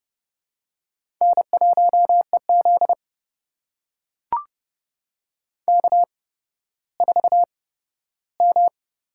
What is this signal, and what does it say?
Morse code, heard as a keyed tone, about 32 wpm, spaced as for 15 wpm, 700 Hz. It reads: N1EZ K4M
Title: USA Call Signs - 4 Characters - 8x Character Spacing - 30wpm